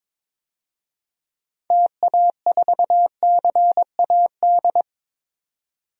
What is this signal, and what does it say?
Morse code, heard as a keyed tone, about 22 wpm, 700 Hz. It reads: TA4CAD
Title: DXCC-100 Call Signs - 6 Characters - 22wpm